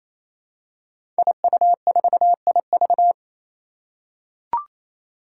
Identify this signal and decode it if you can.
Morse code, heard as a keyed tone, about 27 wpm, 700 Hz. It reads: IU4IV
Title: Five Challenging Letter-Number Combinations 28wpm